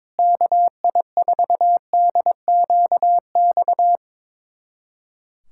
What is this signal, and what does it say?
Morse code, heard as a keyed tone, about 22 wpm, 700 Hz. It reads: KI4DQX
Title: USA Callsign DE Callsign 22wpm